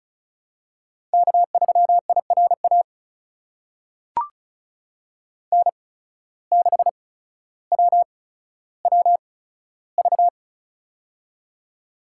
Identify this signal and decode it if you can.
Morse code, heard as a keyed tone, about 36 wpm, spaced as for 15 wpm, 700 Hz. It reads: K3IRA N6WWV
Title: USA Call Signs - 5 Characters - 8x Character Spacing - 35wpm